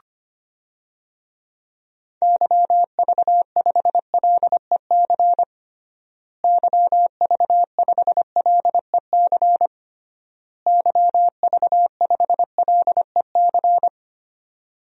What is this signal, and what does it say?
Morse code, heard as a keyed tone, about 25 wpm, 700 Hz. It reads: YV5LEC YV5LEC YV5LEC
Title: DXCC-100 Call Signs - 6 Characters - Repeated 3x - 25wpm